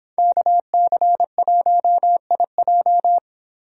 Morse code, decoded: KC1IJ

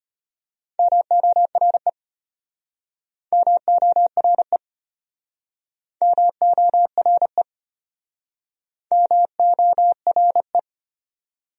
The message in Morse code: MORE MORE MORE MORE

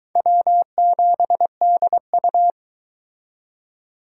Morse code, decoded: W7DU

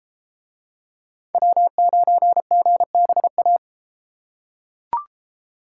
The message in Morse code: W9GBA

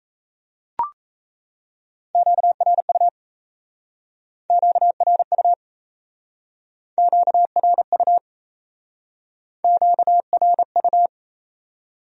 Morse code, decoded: QRU QRU QRU QRU

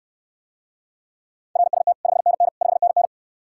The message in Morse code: RST 5NN 5NN

